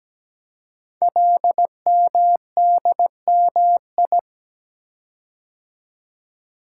Morse code, heard as a keyed tone, about 17 wpm, 700 Hz. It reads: LMDMI
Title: Code Groups - 5 Letters - 17wpm